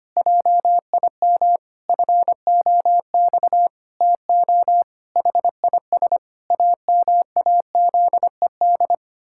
Morse code, heard as a keyed tone, about 25 wpm, 700 Hz. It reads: JIM FOX TO HIS AMAZED
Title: Pangram Challenge v2 - 25wpm